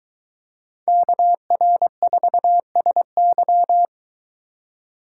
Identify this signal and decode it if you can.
Morse code, heard as a keyed tone, about 23 wpm, 700 Hz. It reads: KR4SY